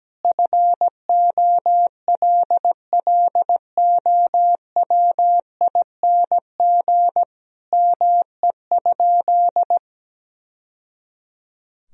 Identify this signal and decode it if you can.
Morse code, heard as a keyed tone, about 17 wpm, 700 Hz. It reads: FOLLOWING ME?
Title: Sentences from Top 1000 Words - Review Part 4 - 17wpm